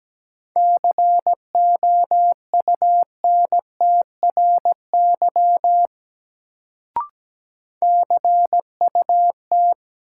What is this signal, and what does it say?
Morse code, heard as a keyed tone, about 17 wpm, 700 Hz. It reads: COUNTRY CUT
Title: Sentences from Top 300 Words - New Word Drill - 17wpm